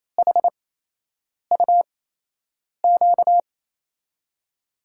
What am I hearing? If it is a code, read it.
Morse code, heard as a keyed tone, about 28 wpm, spaced as for 8 wpm, 700 Hz. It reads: HUQ
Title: USA Call Signs - 6 Characters - 8x Character Spacing - 28wpm